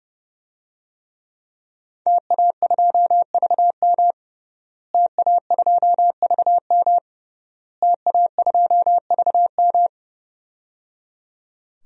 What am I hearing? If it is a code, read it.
Morse code, heard as a keyed tone, about 30 wpm, 700 Hz. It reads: TA2VM TA2VM TA2VM
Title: DXCC-100 Call Signs - 5 Characters - Repeated 3x - 30wpm